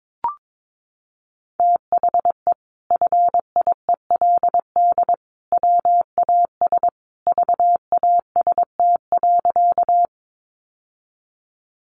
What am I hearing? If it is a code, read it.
Morse code, heard as a keyed tone, about 22 wpm, 700 Hz. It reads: THE FIELD WAS VAST.
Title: Sentences from Top 1300 Words 22wpm